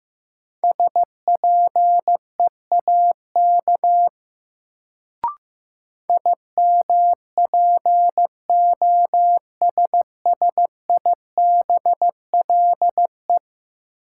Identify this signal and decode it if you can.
Morse code, heard as a keyed tone, about 15 wpm, 700 Hz. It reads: SPEAK IMPOSSIBLE
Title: Sentences from Top 1100 Words - New Word Drill Alt - 15wpm